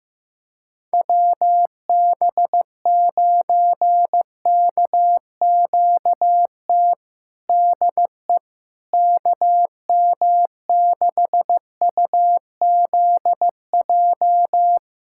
Morse code, heard as a keyed tone, about 15 wpm, 700 Hz. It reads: WB9KQT DE KM6UZJ
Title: USA Callsign DE Callsign - 6 Characters - 15wpm